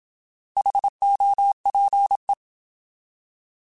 Morse code, heard as a keyed tone, about 27 wpm, 800 Hz. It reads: HOPE
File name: Alphabetical / Alphabetical mp3s / Hs @25